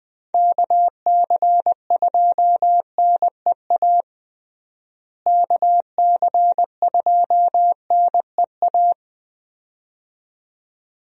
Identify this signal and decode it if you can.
Morse code, heard as a keyed tone, about 21 wpm, 700 Hz. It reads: KC2NEA KC2NEA
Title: USA Call Signs - 6 Characters - Repeated 2x - 20wpm